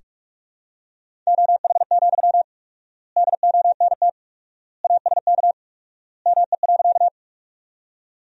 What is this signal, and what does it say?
Morse code, heard as a keyed tone, about 42 wpm, 700 Hz. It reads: OH, DONT ASK ME.
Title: 4x Word-Spacing - Sentences from Top 500 Words - 45wpm